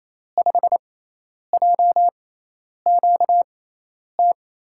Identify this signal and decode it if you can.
Morse code, heard as a keyed tone, about 28 wpm, spaced as for 10 wpm, 700 Hz. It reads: 5JQT
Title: USA Call Signs - 6 Characters - 6x Character Spacing - 28wpm